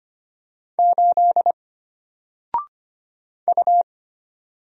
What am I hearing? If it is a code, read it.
Morse code, heard as a keyed tone, about 25 wpm, 700 Hz. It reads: 8 U
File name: Single-Letter-Number-25wpm